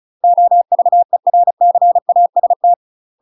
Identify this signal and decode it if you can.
Morse code, heard as a keyed tone, about 36 wpm, 700 Hz. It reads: OVERCAST
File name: UK MASTER QSOs TRAINING / 35WPM / UK MASTER QSOs 0022